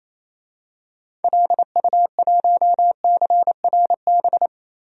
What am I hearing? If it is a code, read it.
Morse code, heard as a keyed tone, about 28 wpm, 700 Hz. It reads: LU1CRB
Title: DXCC-100 Call Signs - 6 Characters - Repeated 2x - 28wpm